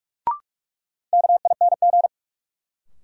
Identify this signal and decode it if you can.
Morse code, heard as a keyed tone, about 46 wpm, 700 Hz. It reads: KING
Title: Sentences from Top 400 Words - New Word Drill - 45wpm